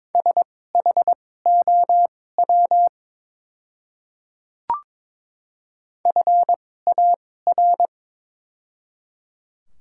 Morse code, heard as a keyed tone, about 22 wpm, spaced as for 16 wpm, 700 Hz. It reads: SHOW FAR
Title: 2x Character Spacing - Top 300 Words - 22wpm